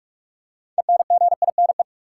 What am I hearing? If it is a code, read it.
Morse code, heard as a keyed tone, about 46 wpm, 700 Hz. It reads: ENGINE
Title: Six Letter Words - 45wpm